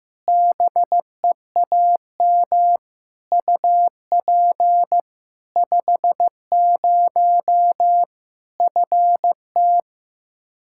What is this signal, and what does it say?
Morse code, heard as a keyed tone, about 15 wpm, 700 Hz. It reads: BEAM UP 50 FT
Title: K1USN SST QSO Practice 15wpm